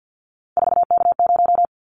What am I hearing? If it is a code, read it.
Morse code, heard as a keyed tone, about 50 wpm, 700 Hz. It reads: 4K0
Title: Callsign Prefixes for Top 100 DXCC - 50wpm